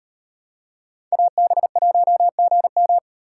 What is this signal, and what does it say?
Morse code, heard as a keyed tone, about 38 wpm, 700 Hz. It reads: AB1GM